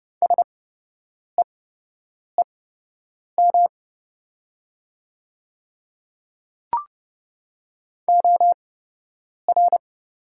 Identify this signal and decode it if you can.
Morse code, heard as a keyed tone, about 29 wpm, spaced as for 8 wpm, 700 Hz. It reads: SEEM OR
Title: 8x Character Spacing - Top 300 Words - 30wpm